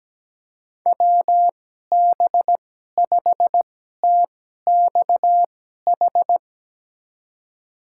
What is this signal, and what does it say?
Morse code, heard as a keyed tone, about 17 wpm, spaced as for 12 wpm, 700 Hz. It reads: WB5TXH
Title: USA Call Signs - 2x Character Spacing - 17wpm